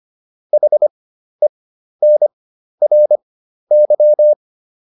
Morse code, heard as a keyed tone, about 25 wpm, spaced as for 12 wpm, 600 Hz. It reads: HENRY